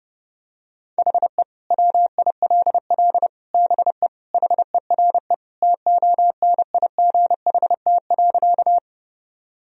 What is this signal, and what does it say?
Morse code, heard as a keyed tone, about 31 wpm, 700 Hz. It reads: HE WILL BE HERE TONIGHT.
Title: Sentences from Top 1000 Words - Review Part 3 - 30wpm